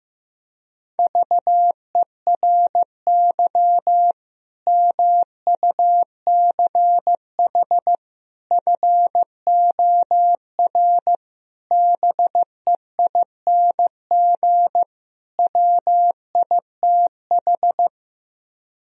Morse code, heard as a keyed tone, about 15 wpm, 700 Hz. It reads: VERY MUCH FOR BEING WITH
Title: Speed-Racing - Sets of 5 Words 15wpm